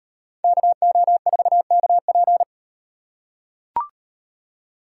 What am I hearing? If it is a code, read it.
Morse code, heard as a keyed tone, about 35 wpm, 700 Hz. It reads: KO4KP